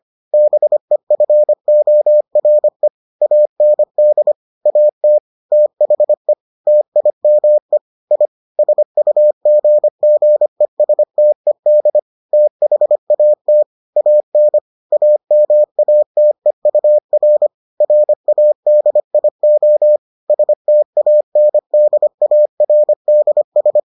Morse code, heard as a keyed tone, about 25 wpm, 600 Hz. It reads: BEFORE AND AT THE TIME I SUGGESTED THAT AN AMATEUR RADIO STANDARDS